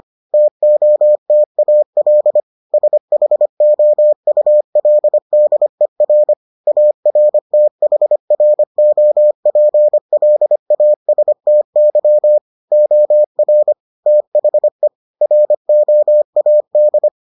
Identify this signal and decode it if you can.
Morse code, heard as a keyed tone, about 25 wpm, 600 Hz. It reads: TOTAL SHOULDER ARTHROPLASTY OR THE ROAD